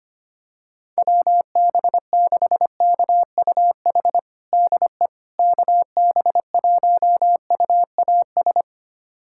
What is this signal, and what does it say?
Morse code, heard as a keyed tone, about 25 wpm, 700 Hz. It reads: WB6KUH DE KB1UAS
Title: USA Callsign DE Callsign - 6 Characters - 25wpm